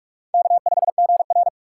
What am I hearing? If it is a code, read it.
Morse code, heard as a keyed tone, about 46 wpm, 700 Hz. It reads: K5GR
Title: USA Call Signs - 4x Character Spacing - 45wpm